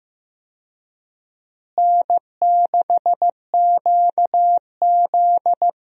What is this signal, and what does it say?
Morse code, heard as a keyed tone, about 15 wpm, 700 Hz. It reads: N6QZ
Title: USA Call Signs - 4 Characters - Repeated 2x - 15wpm